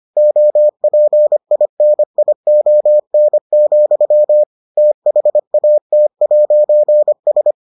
Morse code, heard as a keyed tone, about 25 wpm, 600 Hz. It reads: OPINION, THAT'S